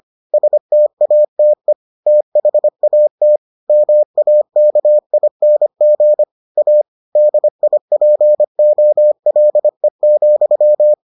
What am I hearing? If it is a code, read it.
Morse code, heard as a keyed tone, about 25 wpm, 600 Hz. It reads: STATE THAT MAKING A DIPOLE,